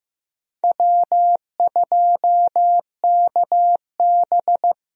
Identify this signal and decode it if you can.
Morse code, heard as a keyed tone, about 16 wpm, 700 Hz. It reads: W2KB
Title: USA Call Signs - 4 Characters - Repeated 2x - 15wpm